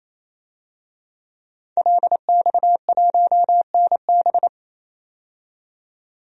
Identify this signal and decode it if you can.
Morse code, heard as a keyed tone, about 28 wpm, 700 Hz. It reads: LX1NB
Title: DXCC-100 Call Signs - 5 Characters - 28wpm